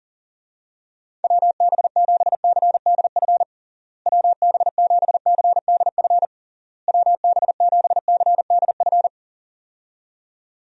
Morse code, heard as a keyed tone, about 40 wpm, 700 Hz. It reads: WB7CDF WB7CDF WB7CDF
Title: USA Call Signs - 6 Characters - Repeated 3x - 40wpm